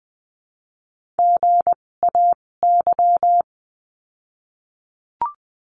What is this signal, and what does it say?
Morse code, heard as a keyed tone, about 20 wpm, spaced as for 16 wpm, 700 Hz. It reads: GAY
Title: 256 Lesson 42a - Words 12-20wpm